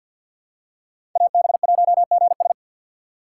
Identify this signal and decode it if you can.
Morse code, heard as a keyed tone, about 52 wpm, 700 Hz. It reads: AB1GS
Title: USA Call Signs - 5 Characters - 3x Character Spacing - 50wpm